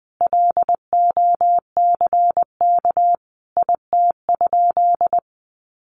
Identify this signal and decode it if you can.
Morse code, heard as a keyed tone, about 20 wpm, 700 Hz. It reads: LOCK IT?
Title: Sentences from Top 1400 Words 20wpm